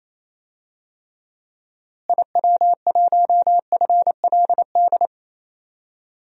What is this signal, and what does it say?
Morse code, heard as a keyed tone, about 28 wpm, 700 Hz. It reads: IW1FLD
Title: DXCC-100 Call Signs - 6 Characters - Repeated 3x - 28wpm